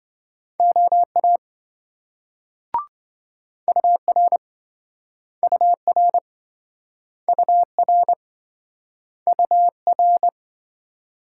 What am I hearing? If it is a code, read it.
Morse code, heard as a keyed tone, about 27 wpm, 700 Hz. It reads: OA UR UR UR UR